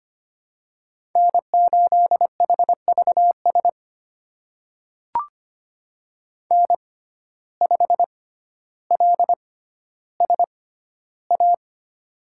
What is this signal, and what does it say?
Morse code, heard as a keyed tone, about 25 wpm, spaced as for 13 wpm, 700 Hz. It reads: N8HVS N5LSA
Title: USA Call Signs - 6x Character Spacing - 25wpm